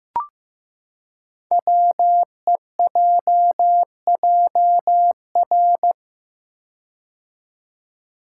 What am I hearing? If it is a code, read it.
Morse code, heard as a keyed tone, about 15 wpm, 700 Hz. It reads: WEJJR
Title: Code Groups - 5 Letters - 15wpm